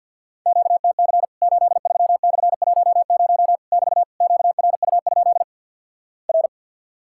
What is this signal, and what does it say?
Morse code, heard as a keyed tone, about 48 wpm, 700 Hz. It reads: QTC 83/10 <BT> QRU? R
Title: Worked All Europe DX Contest - QTC Exchange with Repeats - 50wpm